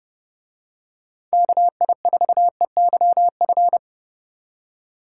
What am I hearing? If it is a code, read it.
Morse code, heard as a keyed tone, about 27 wpm, 700 Hz. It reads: KI4EYF